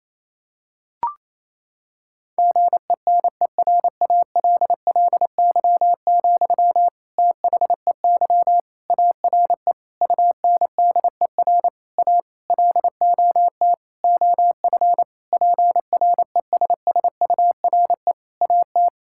Morse code, heard as a keyed tone, about 28 wpm, 700 Hz. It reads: GENERALLY, THEY ARE UNDER A LOT OF PRESSURE AT
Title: Sentences from Top 1000 Words - Review Part 6 - 28wpm